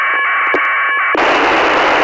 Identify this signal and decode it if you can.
Packet radio (AX.25, as in APRS).
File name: ISS-01